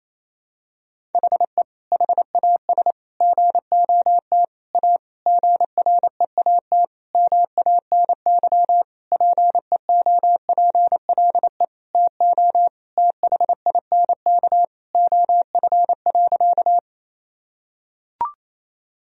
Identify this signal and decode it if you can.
Morse code, heard as a keyed tone, about 27 wpm, 700 Hz. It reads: HE HAS GOT A GREAT MANY PEOPLE TO THINK OF.
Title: Sentences from Top 1000 Words - Review Part 1 - 28wpm